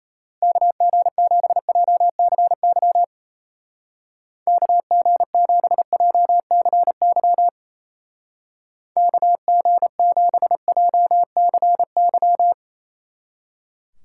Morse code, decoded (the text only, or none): KG7JCY KG7JCY KG7JCY